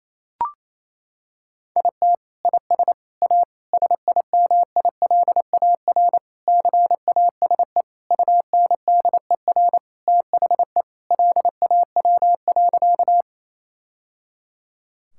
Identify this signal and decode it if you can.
Morse code, heard as a keyed tone, about 28 wpm, 700 Hz. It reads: IT IS A SIMILAR CASE UNDER THE LAW.
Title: Sentences from Top 1000 Words - Review Part 5 - 28wpm